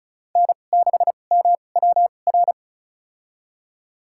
Morse code, heard as a keyed tone, about 36 wpm, spaced as for 25 wpm, 700 Hz. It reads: N6MWR
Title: USA Call Signs - 5 Characters - 2x Character Spacing - 35wpm